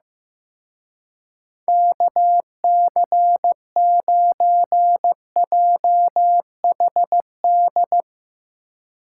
Morse code, heard as a keyed tone, about 16 wpm, 700 Hz. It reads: KC9JHD
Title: Speed Racing - USA 2x3 Call Signs 15wpm